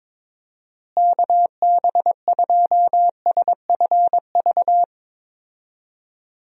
Morse code, heard as a keyed tone, about 22 wpm, 700 Hz. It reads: KB2SFV
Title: USA Call Signs - 6 Characters - Repeated 2x - 22wpm